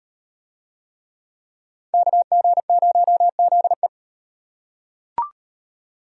Morse code, heard as a keyed tone, about 32 wpm, 700 Hz. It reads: KG0ZE